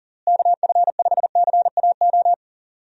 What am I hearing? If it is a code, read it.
Morse code, heard as a keyed tone, about 40 wpm, 700 Hz. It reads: KF5CAO
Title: USA Call Signs - 6 Characters - Repeated 3x - 40wpm